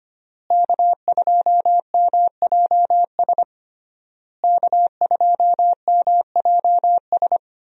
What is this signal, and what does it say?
Morse code, heard as a keyed tone, about 25 wpm, 700 Hz. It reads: K2MJS K2MJS
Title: USA Call Signs - 5 Characters - Repeated 3x - 25wpm